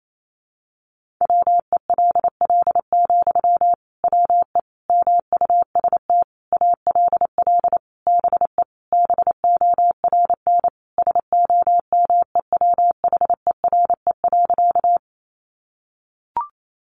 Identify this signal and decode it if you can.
Morse code, heard as a keyed tone, about 28 wpm, 700 Hz. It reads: WELL, WE MUST ALL BE BORN SOMEWHERE.
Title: Sentences from Top 1400 Words 28wpm